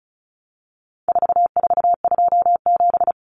Morse code, heard as a keyed tone, about 34 wpm, 700 Hz. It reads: 4427
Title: Numbers - 4-Digit 35wpm